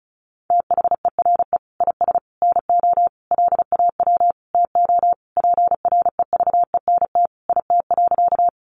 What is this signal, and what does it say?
Morse code, heard as a keyed tone, about 34 wpm, 700 Hz. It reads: THERE IS NO LAW TO PREVENT IT.
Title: Sentences from Top 1100 Words 35wpm